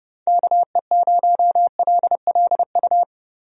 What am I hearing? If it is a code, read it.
Morse code, heard as a keyed tone, about 30 wpm, 700 Hz. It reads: KE0LLU
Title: USA Call Signs - 6 Characters - Repeated 3x - 30wpm